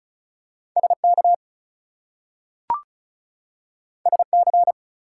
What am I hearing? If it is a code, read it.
Morse code, heard as a keyed tone, about 36 wpm, 700 Hz. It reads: SK SC
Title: US States and Canadian Provinces 35wpm